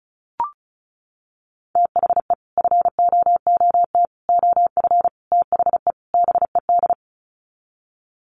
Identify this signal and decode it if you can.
Morse code, heard as a keyed tone, about 34 wpm, 700 Hz. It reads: THE FOOT OF THE BED
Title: Sets of 5 Words Encore2 35wpm